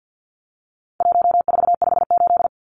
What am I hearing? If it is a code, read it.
Morse code, heard as a keyed tone, about 50 wpm, 700 Hz. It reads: 1458
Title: Numbers - 4-Digit 50wpm